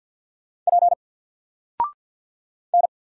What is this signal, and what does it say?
Morse code, heard as a keyed tone, about 50 wpm, 700 Hz. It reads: P N
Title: Single Character (Letter:Number:Punctuation:Prosign) 50wpm